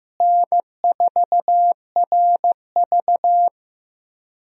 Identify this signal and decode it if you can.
Morse code, heard as a keyed tone, about 15 wpm, 700 Hz. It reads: N4RV
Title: USA Call Signs - 4 Characters - Repeated 3x - 15wpm